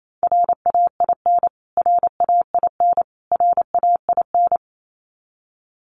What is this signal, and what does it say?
Morse code, heard as a keyed tone, about 28 wpm, 700 Hz. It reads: RAIN RAIN RAIN
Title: Top 300 Words Repeated 28wpm